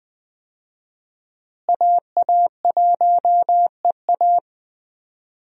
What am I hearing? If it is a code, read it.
Morse code, heard as a keyed tone, about 20 wpm, 700 Hz. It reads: AA1EA
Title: USA Call Signs - 5 Characters - Repeated 3x - 20wpm